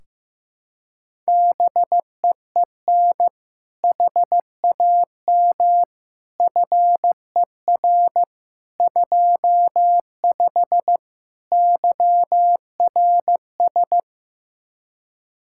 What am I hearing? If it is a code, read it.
Morse code, heard as a keyed tone, about 15 wpm, 700 Hz. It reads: BEEN HAM FER 25 YRS
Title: K1USN SST QSO Practice 15wpm